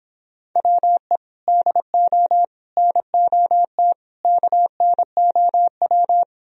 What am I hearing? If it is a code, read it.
Morse code, heard as a keyed tone, about 29 wpm, 700 Hz. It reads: WE DO NOT KNOW